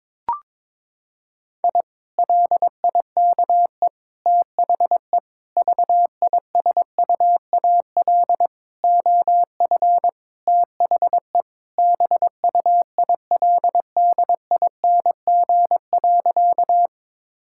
Morse code, heard as a keyed tone, about 22 wpm, 700 Hz. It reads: I LIKE THE VISUAL OF THE BUILDING.
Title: Sentences from Top 1000 Words - Review Part 6 - 22wpm